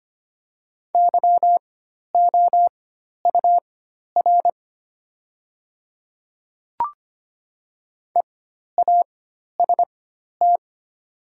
Morse code, heard as a keyed tone, about 25 wpm, spaced as for 12 wpm, 700 Hz. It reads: YOUR EAST
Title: 4x Character Spacing - Top 500 Words - 25wpm